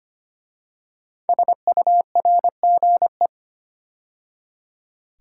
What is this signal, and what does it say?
Morse code, heard as a keyed tone, about 25 wpm, 700 Hz. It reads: SURGE
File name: 5 Letter Words - First 3 Letters Given - 25wpm